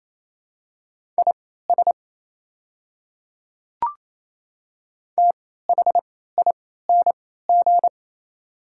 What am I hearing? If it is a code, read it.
Morse code, heard as a keyed tone, about 28 wpm, spaced as for 16 wpm, 700 Hz. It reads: IS THING
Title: 3x Character Spacing - Top 100 Words - 28wpm